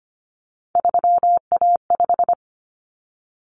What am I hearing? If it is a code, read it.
Morse code, heard as a keyed tone, about 25 wpm, 700 Hz. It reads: 3A5
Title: Common Cut Numbers - 3-Digit 25wpm